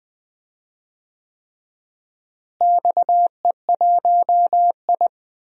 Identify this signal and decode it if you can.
Morse code, heard as a keyed tone, about 21 wpm, 700 Hz. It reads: XE1I
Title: DXCC-100 Call Signs - 4 Characters - Repeated 3x - 20wpm